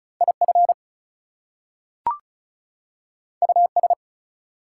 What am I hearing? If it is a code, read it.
Morse code, heard as a keyed tone, about 36 wpm, 700 Hz. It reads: IF US
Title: Two Letter Words - 35wpm